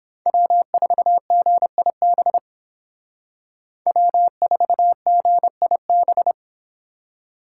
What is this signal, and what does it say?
Morse code, heard as a keyed tone, about 31 wpm, 700 Hz. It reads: W4GIB W4GIB